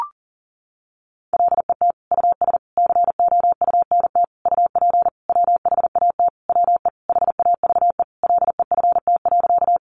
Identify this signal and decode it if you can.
Morse code, heard as a keyed tone, about 40 wpm, 700 Hz. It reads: LET US COUNT UP WHAT WE HAVE LEFT.
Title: Sentences from Top 1500 Words 40wpm